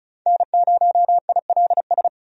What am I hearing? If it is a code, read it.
Morse code, heard as a keyed tone, about 36 wpm, 700 Hz. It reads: N0ILS
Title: USA Call Signs - 5 Characters - 35wpm